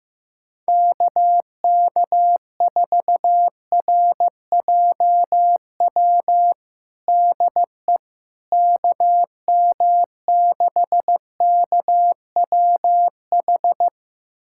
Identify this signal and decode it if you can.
Morse code, heard as a keyed tone, about 16 wpm, 700 Hz. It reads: KK4RJW DE KM6KWH
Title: USA Callsign DE Callsign 15wpm